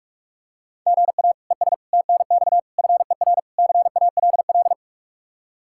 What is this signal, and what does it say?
Morse code, heard as a keyed tone, about 48 wpm, 700 Hz. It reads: GA ES TNX FER CALL